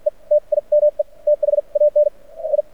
Morse code, heard as a keyed tone, about 52 wpm, spaced as for 42 wpm, 590 Hz. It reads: E TIME THAN I
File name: archived / CRSnet